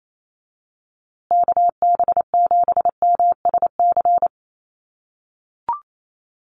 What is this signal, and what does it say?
Morse code, heard as a keyed tone, about 28 wpm, 700 Hz. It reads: KB7MSC
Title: USA Call Signs Repeated 2x 28wpm